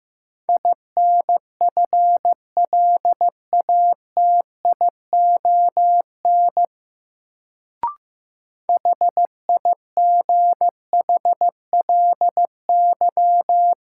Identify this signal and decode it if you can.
Morse code, heard as a keyed tone, about 15 wpm, 700 Hz. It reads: INFLATION HIGHLY
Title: Sentences from Top 1100 Words - New Word Drill Alt - 15wpm